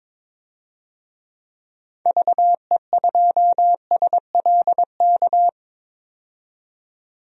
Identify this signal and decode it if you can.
Morse code, heard as a keyed tone, about 22 wpm, 700 Hz. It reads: VE2SLK
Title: DXCC-100 Call Signs - 6 Characters - 22wpm